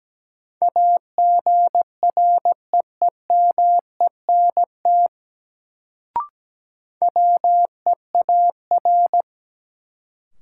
Morse code, heard as a keyed tone, about 17 wpm, 700 Hz. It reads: AGREEMENT WEAR
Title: Sentences from Top 1500 Words - New Word Drill Alt - 17wpm